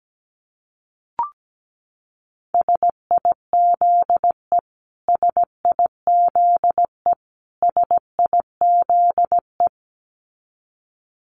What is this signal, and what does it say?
Morse code, heard as a keyed tone, about 17 wpm, 700 Hz. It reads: SIZE SIZE SIZE
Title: Top 500 Words Repeated 17wpm